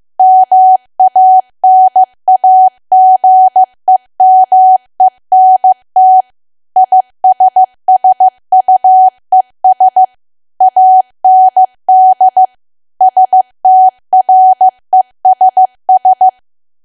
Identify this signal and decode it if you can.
Morse code, heard as a keyed tone, about 15 wpm, 750 Hz. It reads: MANAGEMENT ISSUES AND STRESS
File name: cwa / W1AW-Level2 / 15WPM